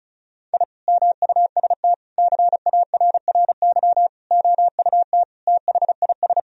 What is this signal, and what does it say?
Morse code, heard as a keyed tone, about 36 wpm, 700 Hz. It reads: I MUST CARRY OUT THIS
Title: Sentences from Top 1000 Words - Review Part 2 - 35wpm